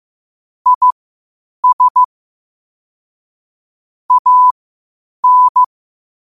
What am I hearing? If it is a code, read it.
Morse code, heard as a keyed tone, about 15 wpm, spaced as for 8 wpm, 1000 Hz. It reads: IS AN
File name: qotdfiles / 8wpm